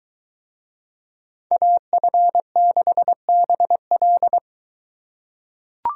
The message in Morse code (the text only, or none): AF6BL